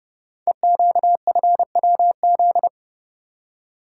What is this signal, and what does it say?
Morse code, heard as a keyed tone, about 30 wpm, 700 Hz. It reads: EQFWZ
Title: Code Groups - 5 Letter:Numbers - 30wpm